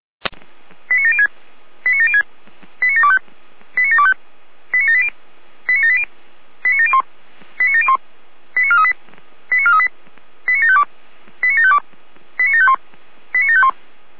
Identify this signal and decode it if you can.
Five-tone selective call (ZVEI), heard as quick runs of five tones, tones of about 80 ms. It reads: 89706 89706 89725 89725 89780 89780 89701 89701 89438 89438 89752 89752 89751 89751